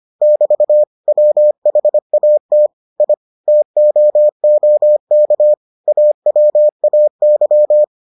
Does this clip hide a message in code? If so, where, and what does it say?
Yes: Morse code from 0.2 to 7.8 s, heard as a keyed tone, about 25 wpm, 600 Hz. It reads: <BT> WHAT I TOOK AWAY